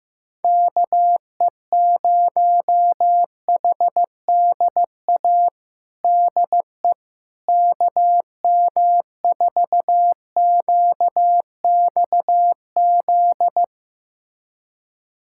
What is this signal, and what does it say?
Morse code, heard as a keyed tone, about 15 wpm, 700 Hz. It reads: KE0HDA DE KM4QXZ
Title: USA Callsign DE Callsign - 6 Characters - 15wpm